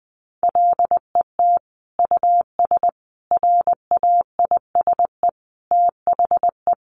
Morse code, heard as a keyed tone, about 20 wpm, 700 Hz. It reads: LET US RAISE THE
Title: Sentences from Top 1400 Words 20wpm